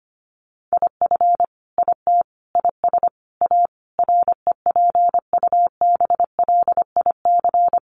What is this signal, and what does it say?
Morse code, heard as a keyed tone, about 25 wpm, 700 Hz. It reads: IF IT IS A REPUBLIC
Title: Sentences from Top 1500 Words 25wpm